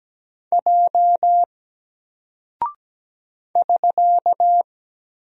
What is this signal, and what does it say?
Morse code, heard as a keyed tone, about 17 wpm, 700 Hz. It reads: J <SK>
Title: Single Character (Letter:Number:Punctuation:Prosign) Alt 17wpm